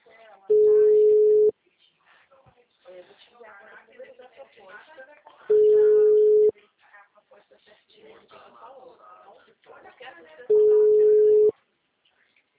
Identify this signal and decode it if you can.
Ringback tone (Europe, 425 Hz), with speech.
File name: out- -2003- - - .65859